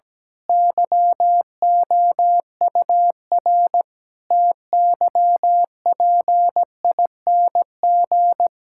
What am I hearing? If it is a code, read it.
Morse code, heard as a keyed tone, about 17 wpm, 700 Hz. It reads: YOUR TYPING
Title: Pangram Challenge v2 - 17wpm